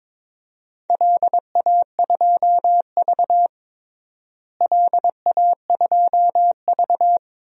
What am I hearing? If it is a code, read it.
Morse code, heard as a keyed tone, about 22 wpm, 700 Hz. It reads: LA2V LA2V
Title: DXCC-100 Call Signs - 4 Characters - Repeated 3x - 22wpm